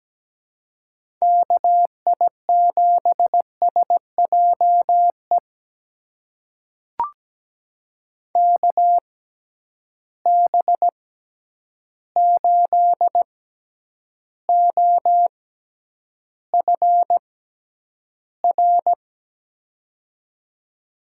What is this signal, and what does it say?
Morse code, heard as a keyed tone, about 17 wpm, spaced as for 9 wpm, 700 Hz. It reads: KI7SJE KB8OFR
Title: USA Call Signs - 6x Character Spacing - 17wpm